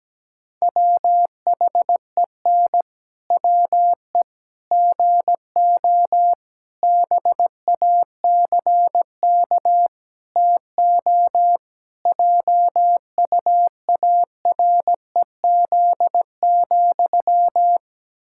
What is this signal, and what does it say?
Morse code, heard as a keyed tone, about 17 wpm, 700 Hz. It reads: WHEN WE GO BACK TO JUAREZ,
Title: Pangram Challenge v2 - 17wpm